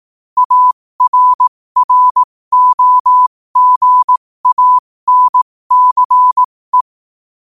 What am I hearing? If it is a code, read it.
Morse code, heard as a keyed tone, about 18 wpm, spaced as for 15 wpm, 1000 Hz. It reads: ARROGANCE
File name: qotdfiles / 15wpm